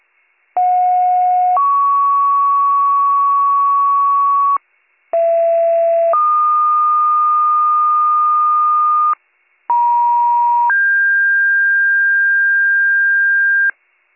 Two-tone page, heard as a one-second tone, then a three-second tone, about 730 then 1120 Hz, about 670 then 1220 Hz, about 950 then 1600 Hz. It is heard from 0.6 to 13.7 s.